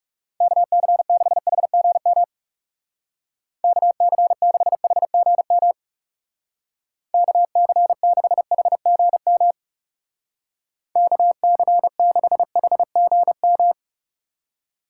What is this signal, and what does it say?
Morse code, heard as a keyed tone, about 37 wpm, 700 Hz. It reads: KC6HGM KC6HGM KC6HGM KC6HGM
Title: Speed Racing - USA 2x3 Call Signs 30wpm